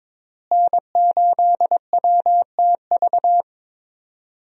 Morse code, heard as a keyed tone, about 22 wpm, 700 Hz. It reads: N8WTV
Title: USA Call Signs - 5 Characters - Repeated 2x - 22wpm